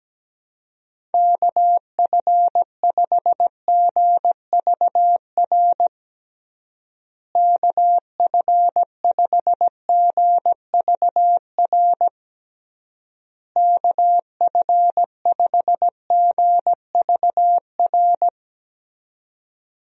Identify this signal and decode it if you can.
Morse code, heard as a keyed tone, about 17 wpm, 700 Hz. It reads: KF5GVR KF5GVR KF5GVR
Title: USA Call Signs - 6 Characters - Repeated 3x - 17wpm